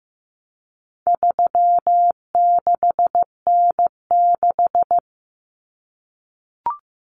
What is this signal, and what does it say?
Morse code, heard as a keyed tone, about 15 wpm, 700 Hz. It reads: 36N6
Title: Common Cut Numbers - 4-Digit 15wpm